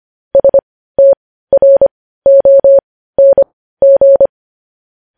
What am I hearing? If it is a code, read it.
Morse code, heard as a keyed tone, about 25 wpm, spaced as for 15 wpm, 550 Hz. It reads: STRONG
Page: News Headlines in Morse Code at 15 WPM